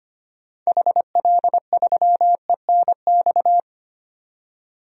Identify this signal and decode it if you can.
Morse code, heard as a keyed tone, about 25 wpm, 700 Hz. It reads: HL3ENX
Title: DXCC-100 Call Signs - 6 Characters - 25wpm